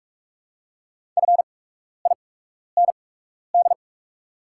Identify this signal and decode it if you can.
Morse code, heard as a keyed tone, about 44 wpm, spaced as for 12 wpm, 700 Hz. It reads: FIND
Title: 8x Character Spacing - Top 100 Words - 45wpm